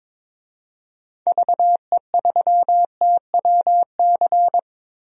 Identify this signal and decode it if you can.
Morse code, heard as a keyed tone, about 22 wpm, 700 Hz. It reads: VE3TWC
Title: DXCC-100 Call Signs - 6 Characters - Repeated 3x - 22wpm